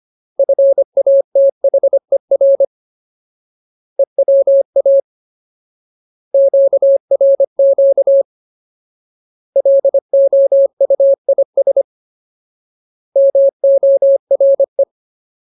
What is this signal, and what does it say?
Morse code, heard as a keyed tone, about 25 wpm, 550 Hz. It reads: FATHER EWA QRQ LOUIS MORE
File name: Ultimate Word List / 25WPM Ultimate